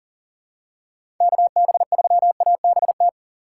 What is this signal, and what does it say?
Morse code, heard as a keyed tone, about 40 wpm, 700 Hz. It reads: KB3ABT
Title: USA Call Signs - 6 Characters - 4x Character Spacing - 40wpm